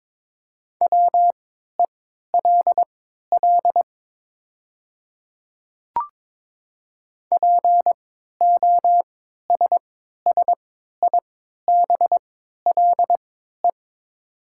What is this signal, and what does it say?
Morse code, heard as a keyed tone, about 22 wpm, spaced as for 13 wpm, 700 Hz. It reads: WELL POSSIBLE
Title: 3x Character Spacing - Top 500 Words - 22wpm